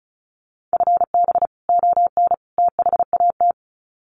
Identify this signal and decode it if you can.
Morse code, heard as a keyed tone, about 34 wpm, 700 Hz. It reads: FB ON THAT
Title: Common QSO Phrases 35wpm